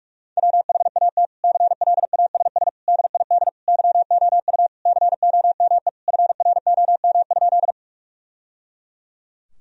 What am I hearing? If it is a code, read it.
Morse code, heard as a keyed tone, about 46 wpm, 700 Hz. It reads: WHAT CLASS DID YOU COME FROM?
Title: Sentences from Top 1000 Words - Review Part 2 - 45wpm